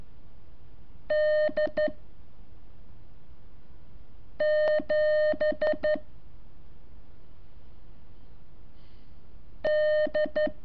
Morse code, heard as a keyed tone, about 13 wpm, 610 Hz. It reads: D 7 D
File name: curso / ejercicio 23